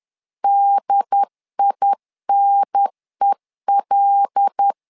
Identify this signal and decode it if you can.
Morse code, heard as a keyed tone, about 10 wpm, 800 Hz. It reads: DINEL
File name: LEZIONI / 5 LBVC / 5 L 10wpm 2259